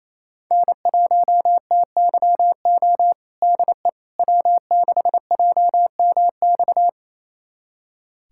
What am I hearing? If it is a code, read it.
Morse code, heard as a keyed tone, about 27 wpm, 700 Hz. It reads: N1TYO DE W6JMX
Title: USA Callsign DE Callsign - 5 Characters - 28wpm